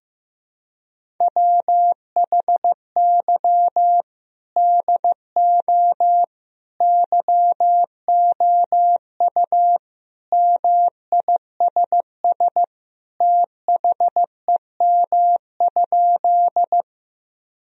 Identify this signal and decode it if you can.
Morse code, heard as a keyed tone, about 15 wpm, 700 Hz. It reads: WHY DO YOU MISS THEM?
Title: Sentences from Top 1000 Words - Review Part 3 - 15wpm